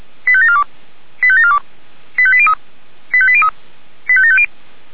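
Five-tone selective call (ZVEI), heard as quick runs of five tones, tones of about 70 ms. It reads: 85742 85742 85803 85803 85750
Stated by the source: Five-tone ZVEI selective call received by SDR.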